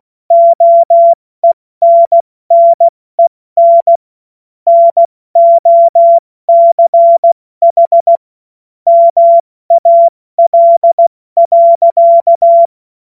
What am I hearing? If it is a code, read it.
Morse code, heard as a keyed tone, about 16 wpm, 680 Hz. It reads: OENNEN NOCH MAL.
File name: Morsekurs / Witze / Witz 12 März 2024 70 F80